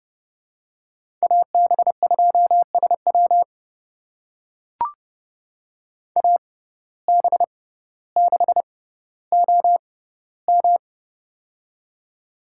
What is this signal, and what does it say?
Morse code, heard as a keyed tone, about 30 wpm, spaced as for 15 wpm, 700 Hz. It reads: AB2SW AB6OM
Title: USA Call Signs - 5 Characters - 6x Character Spacing - 30wpm